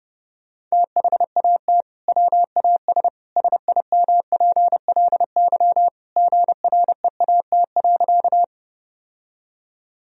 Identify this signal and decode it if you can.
Morse code, heard as a keyed tone, about 30 wpm, 700 Hz. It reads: THAT WAS SIMPLY GREAT.
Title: Sentences from Top 1000 Words - Review Part 4 - 30wpm